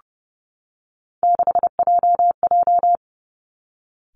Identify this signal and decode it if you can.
Morse code, heard as a keyed tone, about 30 wpm, 700 Hz. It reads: 6JJ
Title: Three Challenging Letter-Number Combinations 30wpm